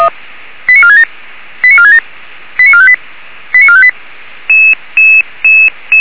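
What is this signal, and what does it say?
Five-tone selective call (ZVEI), heard as quick runs of five tones, tones of about 70 ms. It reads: 89467 89467 89458 89458
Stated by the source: ZVEI five-tone call 89378 received by SDR on channel 464.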